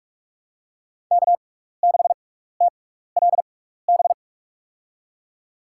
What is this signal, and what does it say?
Morse code, heard as a keyed tone, about 46 wpm, spaced as for 16 wpm, 700 Hz. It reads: K6TLB
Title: USA Call Signs - 5 Characters - 6x Character Spacing - 45wpm